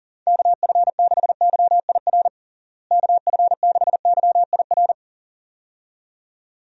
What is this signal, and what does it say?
Morse code, heard as a keyed tone, about 40 wpm, 700 Hz. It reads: KF6YIR KF6YIR
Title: USA Call Signs - 6 Characters - Repeated 2x - 40wpm